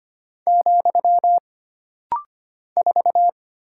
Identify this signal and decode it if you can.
Morse code, heard as a keyed tone, about 25 wpm, 700 Hz. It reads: , 4
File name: Single Character (Letter:Number:Punctuation:Prosign) - Rapid-Fire - 25wpm